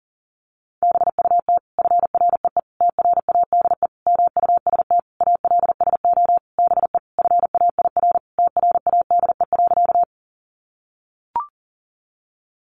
Morse code, heard as a keyed tone, about 40 wpm, 700 Hz. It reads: BUT FREE TRADE MUST ALSO BE FAIR TRADE.
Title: Sentences from Top 1300 Words 40wpm